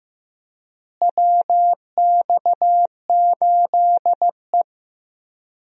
Morse code, heard as a keyed tone, about 15 wpm, 700 Hz. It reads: WX8E
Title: USA Call Signs - 4 Characters - Repeated 2x - 15wpm